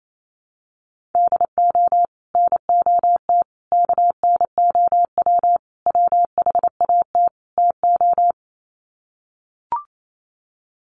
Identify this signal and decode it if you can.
Morse code, heard as a keyed tone, about 28 wpm, 700 Hz. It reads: DO NOT KNOW WHAT TO
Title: Sets of 5 Words 28wpm